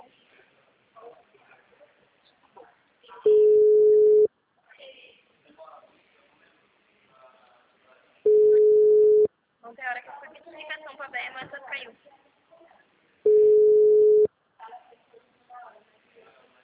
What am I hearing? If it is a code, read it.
Ringback tone (Europe, 425 Hz), with speech.
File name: out- -2018- - - .9092